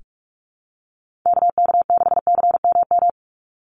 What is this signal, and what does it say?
Morse code, heard as a keyed tone, about 44 wpm, 700 Hz. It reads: KK6CMM
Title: USA Call Signs 45wpm